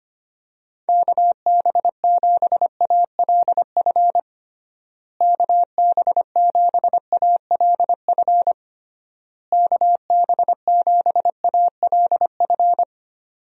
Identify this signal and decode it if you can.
Morse code, heard as a keyed tone, about 25 wpm, 700 Hz. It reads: KB7ALF KB7ALF KB7ALF
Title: USA Call Signs - 6 Characters - Repeated 3x - 25wpm